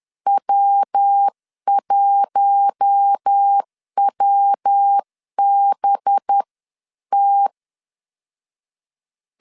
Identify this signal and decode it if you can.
Morse code, heard as a keyed tone, about 10 wpm, spaced as for 9 wpm, 800 Hz. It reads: W1WBT
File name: CW / DET / 1 6 1500C S C 10wpm 4252